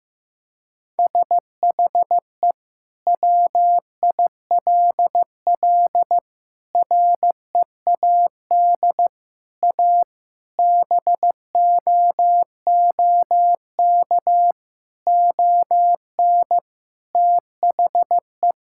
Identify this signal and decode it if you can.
Morse code, heard as a keyed tone, about 15 wpm, 700 Hz. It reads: SHE WILL READ A BOOK ON THE
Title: Sentences from Top 1000 Words - Review Part 6 - 15wpm